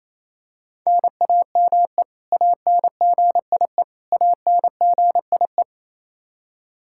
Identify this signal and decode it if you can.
Morse code, heard as a keyed tone, about 27 wpm, 700 Hz. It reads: NAME ANGIE ANGIE
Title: K1USN SST QSO Practice 28wpm